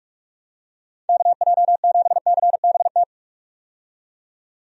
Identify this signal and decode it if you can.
Morse code, heard as a keyed tone, about 46 wpm, 700 Hz. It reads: KJ7CBT
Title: USA Call Signs - 6 Characters - 45wpm